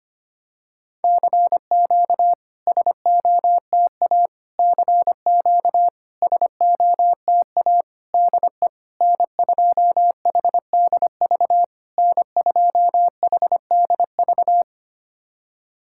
Morse code, heard as a keyed tone, about 25 wpm, 700 Hz. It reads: CQ SOTA CQ SOTA DE N2HDV N2HDV
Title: SOTA QSO Practice 25wpm